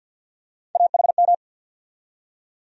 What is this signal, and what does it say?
Morse code, heard as a keyed tone, about 52 wpm, 700 Hz. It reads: AHM
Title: USA Call Sign Postfixes - 3 Letters 50wpm